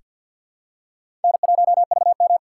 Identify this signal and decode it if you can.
Morse code, heard as a keyed tone, about 48 wpm, 700 Hz. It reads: N1VM
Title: USA Call Signs - 4x Character Spacing - 50wpm